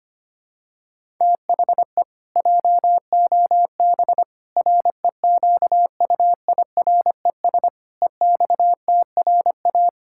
Morse code, heard as a keyed tone, about 25 wpm, 700 Hz. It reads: THE JOB REQUIRES EXTRA
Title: Pangram Challenge v2 - 25wpm